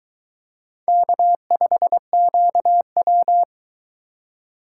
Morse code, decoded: K5QW

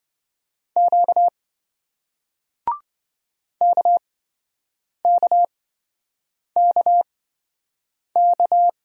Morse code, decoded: Q KKKK

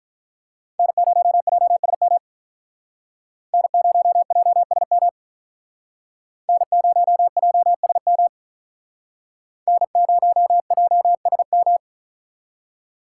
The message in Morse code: N0JSM N0JSM N0JSM N0JSM

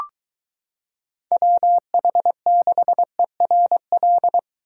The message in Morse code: WH6ERL

